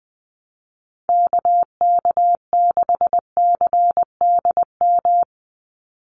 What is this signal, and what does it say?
Morse code, heard as a keyed tone, about 20 wpm, 700 Hz. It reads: KK6CDM
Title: USA Call Signs 20wpm